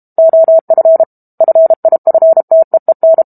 Morse code, heard as a keyed tone, about 32 wpm, 650 Hz. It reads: OF FIFTEEN
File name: Book Two / 06-Chapter 6 - 32 WPM